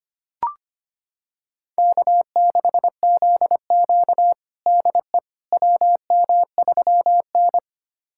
Morse code, heard as a keyed tone, about 25 wpm, 700 Hz. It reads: K6ZQ DE WM3N
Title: USA Callsign DE Callsign - 4 Characters - 25wpm